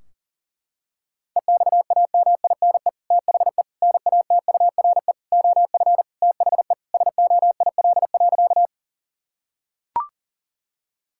Morse code, heard as a keyed tone, about 40 wpm, 700 Hz. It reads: EXAMINE THE NATURE OF THE SOIL.
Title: Sentences from Top 1600 Words 40wpm